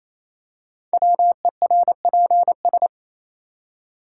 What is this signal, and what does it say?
Morse code, heard as a keyed tone, about 29 wpm, 700 Hz. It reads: WERPS
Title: Code Groups - 5 Letters - 28wpm